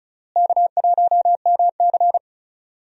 Morse code, heard as a keyed tone, about 36 wpm, 700 Hz. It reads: K1MC
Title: USA Call Signs - 4 Characters - Repeated 3x - 35wpm